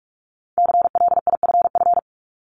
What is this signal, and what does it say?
Morse code, heard as a keyed tone, about 46 wpm, 700 Hz. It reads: CLIFF
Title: Multi-letter Phonemes 45wpm